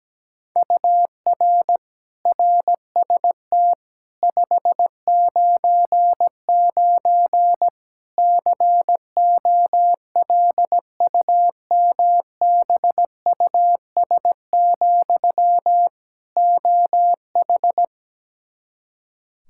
Morse code, heard as a keyed tone, about 17 wpm, 700 Hz. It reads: UR RST 599 COLUMBUS, OH
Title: K1USN SST QSO Practice 17wpm